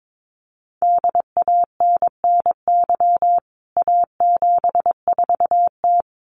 Morse code, heard as a keyed tone, about 22 wpm, 700 Hz. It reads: DANNY A74T
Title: CWT Exchange 22wpm